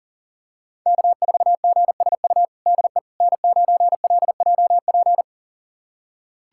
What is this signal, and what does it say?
Morse code, heard as a keyed tone, about 40 wpm, 700 Hz. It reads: K4GSU DE N9LJP
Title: USA Callsign DE Callsign 40wpm